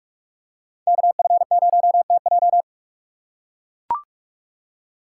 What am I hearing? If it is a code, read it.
Morse code, heard as a keyed tone, about 41 wpm, 700 Hz. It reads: KF0TJ
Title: USA Call Signs - 5 Characters - 6x Character Spacing - 45wpm